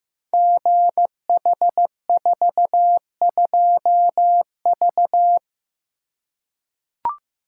Morse code, heard as a keyed tone, about 15 wpm, 700 Hz. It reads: GH42V
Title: Code Groups - 5 Letter:Numbers - 15wpm